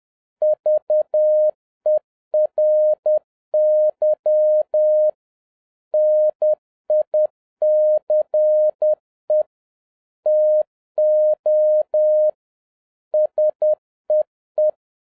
Morse code, heard as a keyed tone, about 10 wpm, 610 Hz. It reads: VERY NICE TO SEE
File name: 2011-07-mp3 / n46 JOHN K3TN 10wpm